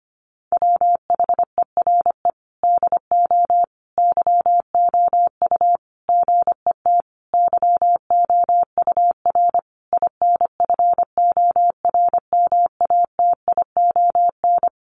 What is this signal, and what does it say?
Morse code, heard as a keyed tone, about 25 wpm, 700 Hz. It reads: WHERE DO YOU GET YOUR INFORMATION
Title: Sentences from Top 600 Words 25wpm